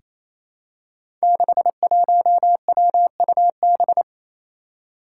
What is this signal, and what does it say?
Morse code, heard as a keyed tone, about 27 wpm, 700 Hz. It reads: 61WUB
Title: Five Challenging Letter-Number Combinations 28wpm